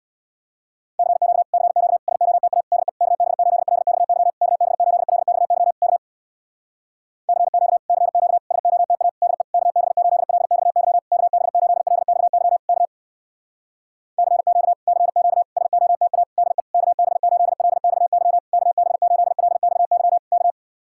Morse code, heard as a keyed tone, about 67 wpm, 700 Hz. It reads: CQ CQ SOTA DE KB8LXY KB8LXY K CQ CQ SOTA DE KB8LXY KB8LXY K CQ CQ SOTA DE KB8LXY KB8LXY K